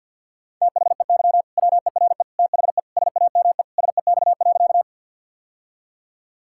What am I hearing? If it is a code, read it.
Morse code, heard as a keyed tone, about 51 wpm, 700 Hz. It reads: THEY WERE THE SAME SEX.
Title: Sentences from Top 1600 Words 50wpm